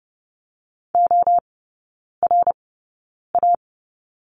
Morse code, heard as a keyed tone, about 30 wpm, spaced as for 9 wpm, 690 Hz. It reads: ORA
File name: Lesson 17e - Letter u - Three Characters 30wpm